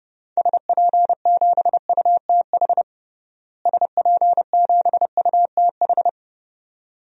Morse code, heard as a keyed tone, about 30 wpm, 700 Hz. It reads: SP7UTH SP7UTH
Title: DXCC-100 Call Signs - 6 Characters - Repeated 2x - 30wpm